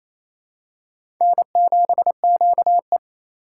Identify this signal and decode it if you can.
Morse code, heard as a keyed tone, about 27 wpm, 700 Hz. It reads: N7QE
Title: USA Call Signs - 3x Character Spacing - 28wpm